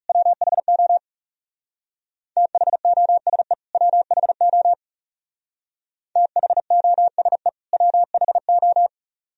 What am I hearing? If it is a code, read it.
Morse code, heard as a keyed tone, about 36 wpm, 700 Hz. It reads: WHO THOSE WHO THOSE WHO
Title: Speed-Racing - Sets of 2 Words 30wpm